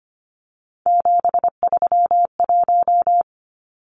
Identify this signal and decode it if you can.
Morse code, heard as a keyed tone, about 25 wpm, 700 Hz. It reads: 731
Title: Numbers - 3-Digit 25wpm